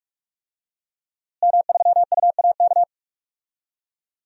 Morse code, heard as a keyed tone, about 43 wpm, 700 Hz. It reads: M3UAK
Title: DXCC-100 Call Signs - 5 Characters - Repeated 2x - 45wpm